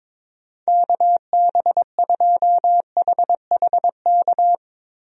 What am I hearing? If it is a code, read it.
Morse code, heard as a keyed tone, about 22 wpm, 700 Hz. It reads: KB2HHK
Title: USA Call Signs - 6 Characters - Repeated 2x - 22wpm